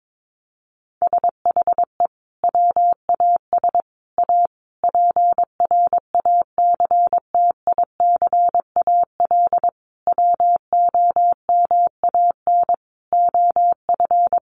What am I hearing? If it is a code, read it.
Morse code, heard as a keyed tone, about 22 wpm, 700 Hz. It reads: SHE WAS A PRACTICAL WOMAN OF
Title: Sentences from Top 1400 Words 22wpm